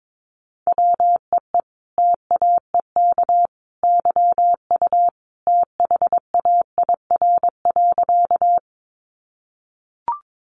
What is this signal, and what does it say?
Morse code, heard as a keyed tone, about 22 wpm, 700 Hz. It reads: WEE TAEK YU THAIR.
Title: A Sound Of Thunder 22wpm